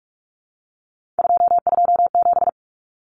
Morse code, heard as a keyed tone, about 44 wpm, 700 Hz. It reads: 227
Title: Common Cut Numbers - 3-Digit 45wpm